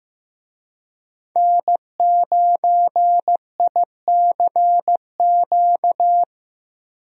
Morse code, heard as a keyed tone, about 15 wpm, 700 Hz. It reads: N9ICQ
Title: USA Call Signs - 5 Characters - Repeated 2x - 15wpm